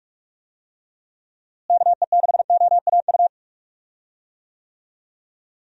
Morse code, heard as a keyed tone, about 47 wpm, 700 Hz. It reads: KE6OAU
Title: USA Call Signs - 6 Characters - Repeated 3x - 45wpm